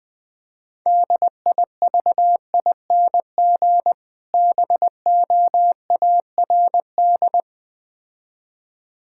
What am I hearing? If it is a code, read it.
Morse code, heard as a keyed tone, about 20 wpm, 700 Hz. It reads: DIVING BOARD
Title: Difficult Words - Speed-Racing with Hints - 20wpm